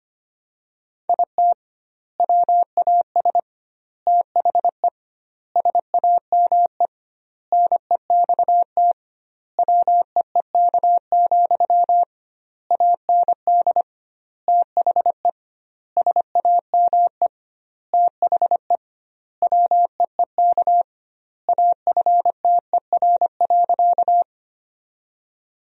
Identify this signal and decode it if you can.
Morse code, heard as a keyed tone, about 25 wpm, 700 Hz. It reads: IT WAS THE SAME NEXT WEEK, AND THE SAME THE WEEK AFTER.
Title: 2x Word-Spacing - Sentences from Top 500 Words - 25wpm